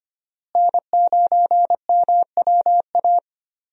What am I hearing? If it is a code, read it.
Morse code, heard as a keyed tone, about 25 wpm, 700 Hz. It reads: N9MWA
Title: USA Call Signs - 5 Characters - Repeated 3x - 25wpm